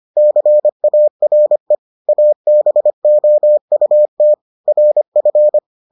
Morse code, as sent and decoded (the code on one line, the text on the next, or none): -.-. .- .-. / .- -... --- ..- - / .-. ..-.
CARE ABOUT RF